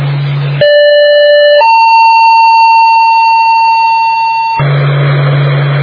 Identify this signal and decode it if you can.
Two-tone page, heard as a one-second tone, then a three-second tone, about 590 then 910 Hz. It is heard from 0.6 to 4.6 s.